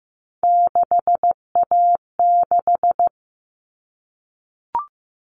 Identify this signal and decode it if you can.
Morse code, heard as a keyed tone, about 15 wpm, 700 Hz. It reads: 6A6
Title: All Cut Numbers - 3-Digit 15wpm